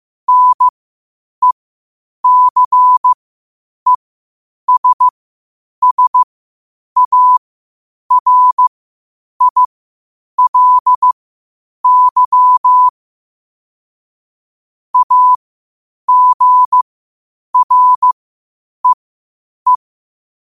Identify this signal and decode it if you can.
Morse code, heard as a keyed tone, about 15 wpm, spaced as for 8 wpm, 1000 Hz. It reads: NECESSARILY AGREE
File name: qotdfiles / 8wpm